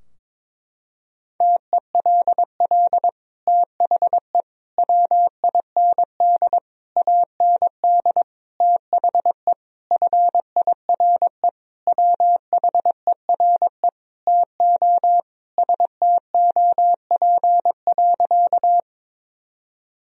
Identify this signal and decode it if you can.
Morse code, heard as a keyed tone, about 22 wpm, 700 Hz. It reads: TELL THE WIND AND THE FIRE WHERE TO STOP.
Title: Sentences from Top 1000 Words - Review Part 2 - 22wpm